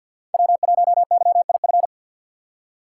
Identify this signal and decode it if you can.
Morse code, heard as a keyed tone, about 47 wpm, 700 Hz. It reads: W1YIF
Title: USA Call Signs - 5 Characters - 6x Character Spacing - 50wpm